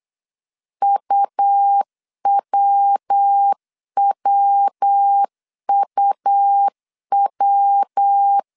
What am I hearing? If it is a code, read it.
Morse code, heard as a keyed tone, about 8 wpm, 800 Hz. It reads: UWWUW